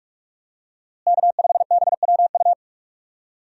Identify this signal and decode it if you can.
Morse code, heard as a keyed tone, about 45 wpm, 700 Hz. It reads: K5BWU
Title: USA Call Signs - 5 Characters - Repeated 2x - 45wpm